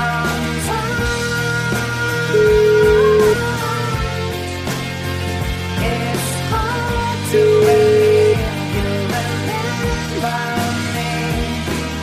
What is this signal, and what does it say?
Ringback tone (Europe, 425 Hz), with noise.